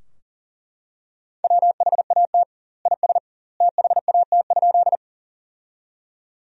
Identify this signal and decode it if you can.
Morse code, heard as a keyed tone, about 40 wpm, 700 Hz. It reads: WHAT IS THAT?
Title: 2x Word-Spacing - Sentences from Top 100 Words - 40wpm